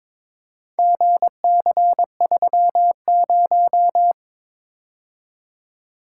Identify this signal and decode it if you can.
Morse code, heard as a keyed tone, about 23 wpm, 700 Hz. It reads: GC30
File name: practice / 22wpm / Four Letter-Number Combinations - 22wpm